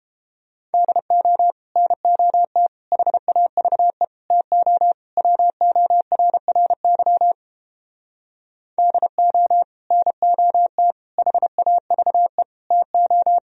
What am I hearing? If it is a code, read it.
Morse code, heard as a keyed tone, about 31 wpm, 700 Hz. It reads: DO NOT HAVE TO WORRY DO NOT HAVE TO